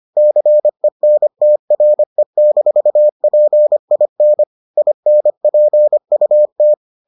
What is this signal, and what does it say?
Morse code, heard as a keyed tone, about 25 wpm, 600 Hz. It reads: CENTRE-PIN INPUT